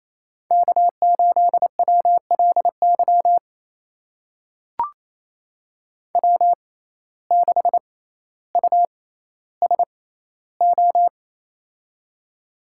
Morse code, heard as a keyed tone, about 28 wpm, spaced as for 14 wpm, 700 Hz. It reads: K8WLY W6USO
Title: USA Call Signs - 5 Characters - 6x Character Spacing - 28wpm